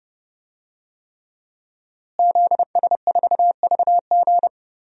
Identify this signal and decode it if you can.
Morse code, heard as a keyed tone, about 30 wpm, 700 Hz. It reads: ZS4VG
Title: DXCC-100 Call Signs - 5 Characters - Repeated 3x - 30wpm